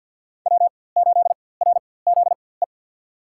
Morse code, heard as a keyed tone, about 47 wpm, spaced as for 23 wpm, 700 Hz. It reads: W8RZE
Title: USA Call Signs - 5 Characters - 4x Character Spacing - 50wpm